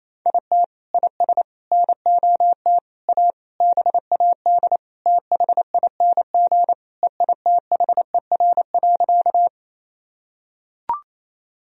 Morse code, heard as a keyed tone, about 28 wpm, 700 Hz. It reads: IT IS NOT A BAD THING EITHER.
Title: Sentences from Top 1000 Words - Review Part 4 - 28wpm